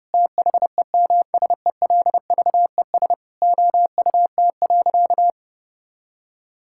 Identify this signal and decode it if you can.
Morse code, heard as a keyed tone, about 32 wpm, 700 Hz. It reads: THEMSELVES OUT.
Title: Sentences from Top 1000 Words - Review Part 5 - 30wpm